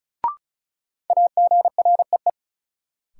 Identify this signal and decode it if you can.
Morse code, heard as a keyed tone, about 34 wpm, 700 Hz. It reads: AGREE
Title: Sets of 5 Words Encore - New Word Drill - 35wpm